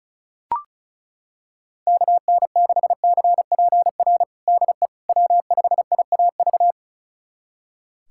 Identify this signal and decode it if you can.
Morse code, heard as a keyed tone, about 34 wpm, 700 Hz. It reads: KN6CPR DE W5IAV
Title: USA Callsign DE Callsign 35wpm